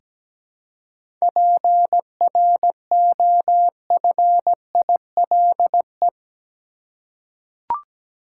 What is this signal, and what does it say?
Morse code, heard as a keyed tone, about 17 wpm, 700 Hz. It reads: PROFILE
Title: Seven Letter Words - 17wpm